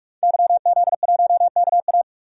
Morse code, heard as a keyed tone, about 45 wpm, 700 Hz. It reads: YZ1KA
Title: DXCC-100 Call Signs - 5 Characters - 45wpm